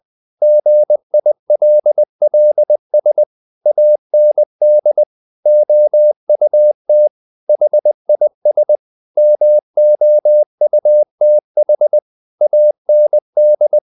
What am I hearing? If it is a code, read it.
Morse code, heard as a keyed tone, about 20 wpm, 600 Hz. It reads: GILLS AND OUT HIS MOUTH AND
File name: the old man and the sea part 2 20wpm